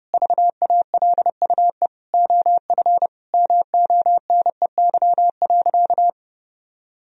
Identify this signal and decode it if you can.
Morse code, heard as a keyed tone, about 30 wpm, 700 Hz. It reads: VALUE OF MONEY.
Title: Sentences from Top 1000 Words - Review Part 3 - 30wpm